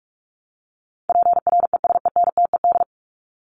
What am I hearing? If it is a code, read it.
Morse code, heard as a keyed tone, about 44 wpm, 700 Hz. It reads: PRESENTED
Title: ED Suffix - Top 100 Words with Base Words 45wpm